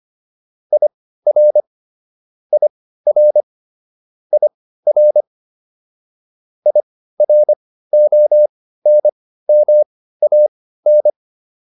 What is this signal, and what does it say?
Morse code, heard as a keyed tone, about 25 wpm, spaced as for 15 wpm, 600 Hz. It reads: IR IR IR IRONMAN